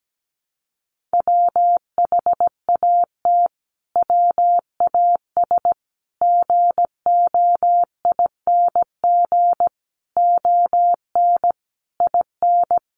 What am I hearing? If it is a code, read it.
Morse code, heard as a keyed tone, about 17 wpm, 700 Hz. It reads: WHAT WAS GOING ON IN
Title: Sets of 5 Words Encore 17wpm